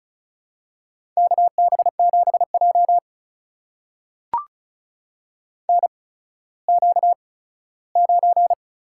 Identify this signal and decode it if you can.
Morse code, heard as a keyed tone, about 36 wpm, spaced as for 17 wpm, 700 Hz. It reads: KB7J NQ9
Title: USA Call Signs - 4 Characters - 8x Character Spacing - 35wpm